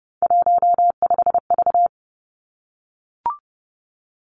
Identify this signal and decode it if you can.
Morse code, heard as a keyed tone, about 30 wpm, 700 Hz. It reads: 15V
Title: Three Challenging Letter-Number Combinations 30wpm